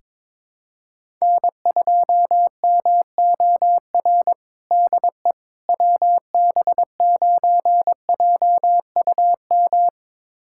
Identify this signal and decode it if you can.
Morse code, heard as a keyed tone, about 22 wpm, 700 Hz. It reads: N2MOR DE WB9JUM
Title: USA Callsign DE Callsign 22wpm